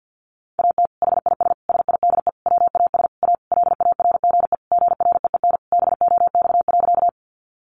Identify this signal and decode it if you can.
Morse code, heard as a keyed tone, about 50 wpm, 700 Hz. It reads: AT HIS SIDE WAS A LARGE GREEN BOX.
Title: Sentences from Top 500 Words 50wpm